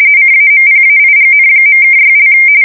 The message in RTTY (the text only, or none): CAMPAIGN.(6)